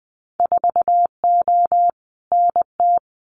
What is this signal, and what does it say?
Morse code, heard as a keyed tone, about 20 wpm, 700 Hz. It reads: 4O NT
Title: Winter Field Day Exchanges Encore 20wpm